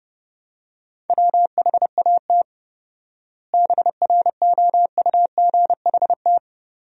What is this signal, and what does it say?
Morse code, heard as a keyed tone, about 30 wpm, 700 Hz. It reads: WHAT BROUGHT
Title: 4x Word-Spacing - Sentences from Top 500 Words - 30wpm